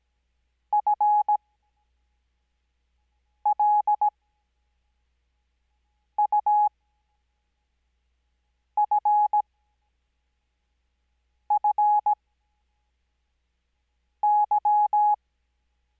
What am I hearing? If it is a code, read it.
Morse code, heard as a keyed tone, about 17 wpm, spaced as for 4 wpm, 840 Hz. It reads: FLUFFY
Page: Farnsworth Spacing: 15WPM Character speed Spaced at 5WPM.